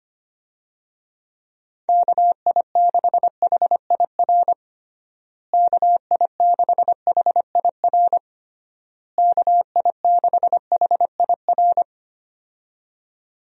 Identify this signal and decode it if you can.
Morse code, heard as a keyed tone, about 25 wpm, 700 Hz. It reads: KI6HIR KI6HIR KI6HIR
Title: USA Call Signs - 6 Characters - Repeated 3x - 25wpm